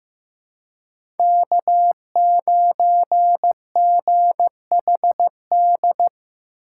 Morse code, heard as a keyed tone, about 15 wpm, 700 Hz. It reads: K9GHD
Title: USA Call Signs - 3x Character Spacing - 15wpm